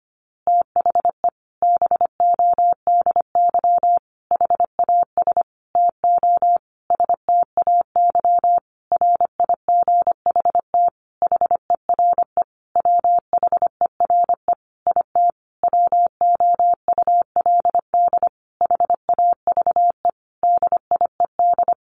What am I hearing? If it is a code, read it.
Morse code, heard as a keyed tone, about 25 wpm, 700 Hz. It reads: THE BODY HAS TO STAY RIGHT HERE WHERE IT WOULD HAVE DIED
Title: A Sound Of Thunder 25wpm